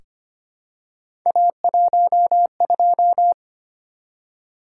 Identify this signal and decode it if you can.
Morse code, heard as a keyed tone, about 25 wpm, 700 Hz. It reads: A12
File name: Three Letter-Number Combinations - 25wpm